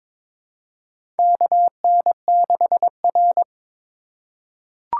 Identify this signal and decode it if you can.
Morse code, heard as a keyed tone, about 23 wpm, 700 Hz. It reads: KN6R
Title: USA Call Signs - 4 Characters - 2x Character Spacing - 22wpm